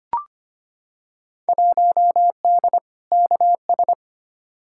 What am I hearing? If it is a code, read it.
Morse code, heard as a keyed tone, about 25 wpm, 700 Hz. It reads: 1D KS
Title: Field Day Exchanges - 25wpm